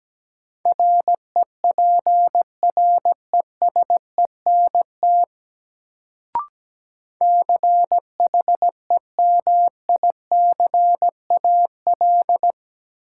Morse code, heard as a keyed tone, about 17 wpm, 700 Hz. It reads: REPRESENT CHEMICAL
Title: Sentences from Top 1200 Words - New Word Drill - 17wpm